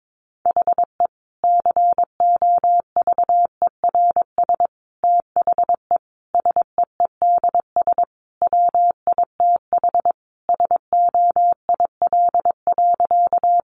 Morse code, heard as a keyed tone, about 22 wpm, 700 Hz. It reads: HE COVERS THE SEEDS WITH SOIL.
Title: Easy Phrases 22wpm